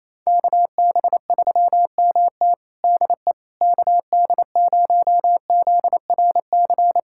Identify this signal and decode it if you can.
Morse code, heard as a keyed tone, about 27 wpm, 700 Hz. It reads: KB3MT DE KD0ZRC
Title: Practice QSO - Exchange 4 - Ending - 28wpm